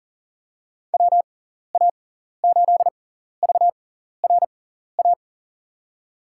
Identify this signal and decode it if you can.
Morse code, heard as a keyed tone, about 40 wpm, spaced as for 14 wpm, 700 Hz. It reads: WA8VRA
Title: USA Call Signs - 6 Characters - 6x Character Spacing - 40wpm